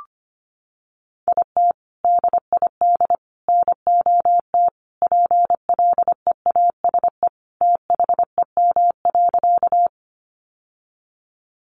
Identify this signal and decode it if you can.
Morse code, heard as a keyed tone, about 25 wpm, 700 Hz. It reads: IT DID NOT PLEASE THEM.
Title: Sentences from Top 1500 Words 25wpm